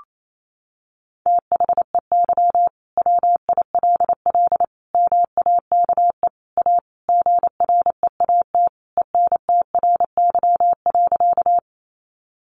Morse code, heard as a keyed tone, about 28 wpm, 700 Hz. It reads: THEY WILL MAKE A GREAT ENTRY.
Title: Sentences from Top 1500 Words 28wpm